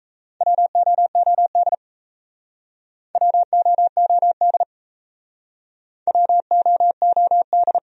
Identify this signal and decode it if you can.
Morse code, heard as a keyed tone, about 36 wpm, 700 Hz. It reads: WOOD WOOD WOOD